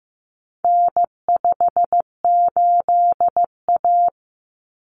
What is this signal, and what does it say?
Morse code, heard as a keyed tone, about 15 wpm, 700 Hz. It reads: N58A
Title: Common Cut Numbers - 4-Digit 15wpm